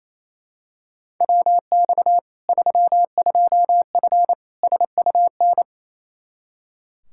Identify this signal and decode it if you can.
Morse code, heard as a keyed tone, about 28 wpm, 700 Hz. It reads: WX 32F SUN
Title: Practice QSO - Exchange 2 - Info - 28wpm